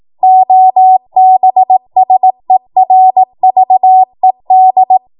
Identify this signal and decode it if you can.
Morse code, heard as a keyed tone, about 18 wpm, 750 Hz. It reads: OBSERVED
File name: Morse / 18WPM